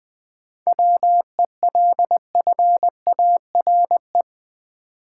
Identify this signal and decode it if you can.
Morse code, heard as a keyed tone, about 20 wpm, 700 Hz. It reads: WELFARE
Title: 7 Letter Words - First 3 Letters Given - 20wpm